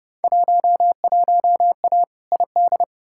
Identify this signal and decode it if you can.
Morse code, heard as a keyed tone, about 31 wpm, 700 Hz. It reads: 11A ID
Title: Field Day Exchanges Encore - 30wpm